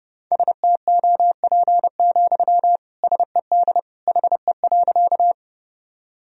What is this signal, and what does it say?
Morse code, heard as a keyed tone, about 32 wpm, 700 Hz. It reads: STOP, SED HE.
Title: Sentences from Top 300 Words and Abbreviations 30wpm